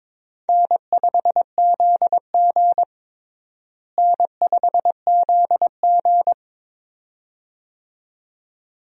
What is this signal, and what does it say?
Morse code, heard as a keyed tone, about 22 wpm, 700 Hz. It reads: N5ZG N5ZG
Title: USA Call Signs - 4 Characters - Repeated 2x - 22wpm